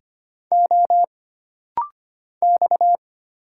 Morse code, heard as a keyed tone, about 25 wpm, 700 Hz. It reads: O X
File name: Single Character (Letter:Number:Punctuation:Prosign) - Rapid-Fire - 25wpm